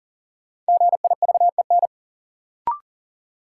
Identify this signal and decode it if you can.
Morse code, heard as a keyed tone, about 38 wpm, 700 Hz. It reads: GIVEN
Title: Sentences from Top 1400 Words - New Word Drill - 40wpm